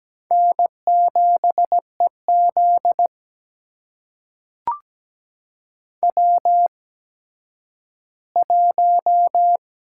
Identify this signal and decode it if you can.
Morse code, heard as a keyed tone, about 17 wpm, spaced as for 10 wpm, 700 Hz. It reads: N7EZ W1
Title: USA Call Signs - 4 Characters - 8x Character Spacing - 17wpm